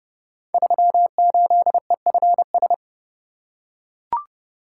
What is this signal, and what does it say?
Morse code, heard as a keyed tone, about 30 wpm, 700 Hz. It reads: 38EFS
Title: Code Groups - 5 Letter:Numbers - 30wpm